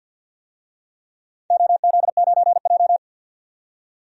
Morse code, heard as a keyed tone, about 53 wpm, 700 Hz. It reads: OZ9J
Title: DXCC-100 Call Signs - 4 Characters - 50wpm